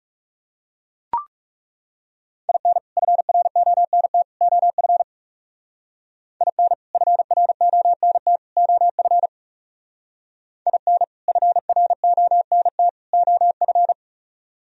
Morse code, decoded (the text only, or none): IN FRONT OF IN FRONT OF IN FRONT OF